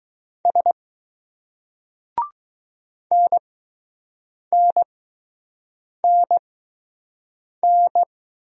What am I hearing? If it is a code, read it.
Morse code, heard as a keyed tone, about 20 wpm, spaced as for 6 wpm, 700 Hz. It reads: S NNNN